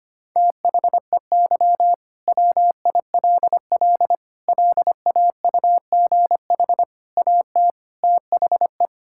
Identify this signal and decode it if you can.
Morse code, heard as a keyed tone, about 25 wpm, 700 Hz. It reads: THEY WILL LAUGH AT THE
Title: Sentences from Top 1000 Words - Review Part 6 - 25wpm